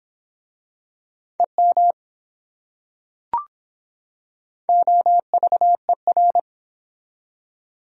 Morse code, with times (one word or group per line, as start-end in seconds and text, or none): 1.4-1.9 EM
4.7-6.4 OVER